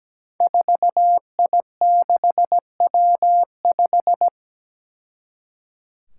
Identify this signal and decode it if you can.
Morse code, heard as a keyed tone, about 17 wpm, 700 Hz. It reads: 4I6W5
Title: Five Challenging Letter-Number Combinations 17wpm